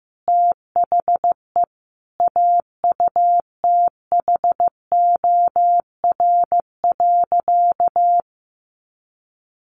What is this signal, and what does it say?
Morse code, heard as a keyed tone, about 15 wpm, 700 Hz. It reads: THE AUTHOR.
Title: Sentences from Top 1200 Words 15wpm